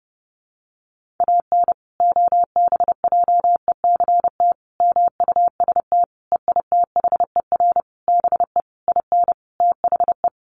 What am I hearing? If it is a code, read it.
Morse code, heard as a keyed tone, about 30 wpm, 700 Hz. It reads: AN OBJECT MUST EITHER BE IN THE
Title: Sentences from Top 1400 Words 30wpm